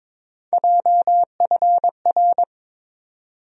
Morse code, heard as a keyed tone, about 22 wpm, 700 Hz. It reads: JFR
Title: USA Call Sign Postfixes - 3 Letters 22wpm